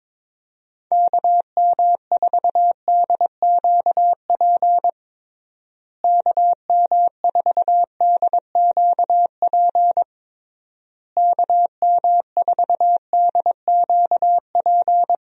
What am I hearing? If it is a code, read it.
Morse code, heard as a keyed tone, about 22 wpm, 700 Hz. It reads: KM4DQP KM4DQP KM4DQP
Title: USA Call Signs - 6 Characters - Repeated 3x - 22wpm